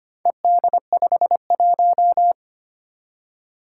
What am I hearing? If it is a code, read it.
Morse code, heard as a keyed tone, about 25 wpm, 700 Hz. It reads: ED51
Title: Four Challenging Letter-Number Combinations 25wpm